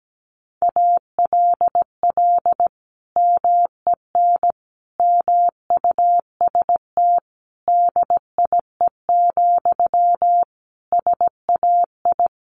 Morse code, heard as a keyed tone, about 17 wpm, 700 Hz. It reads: ALL MEN MUST DIE, SAI
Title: Sentences from Top 1300 Words 17wpm